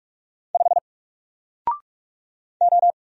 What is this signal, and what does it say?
Morse code, heard as a keyed tone, about 46 wpm, 700 Hz. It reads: 5 O
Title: Single Letter-Number 45wpm